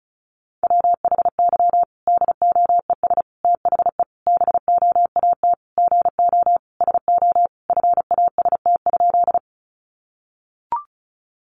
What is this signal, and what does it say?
Morse code, heard as a keyed tone, about 34 wpm, 700 Hz. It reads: WHY DOES THE BOAT GO SO FAST?
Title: Sentences from Top 500 Words 35wpm